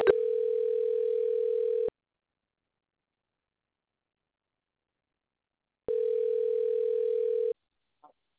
Ringback tone (North America), 440 + 480 Hz.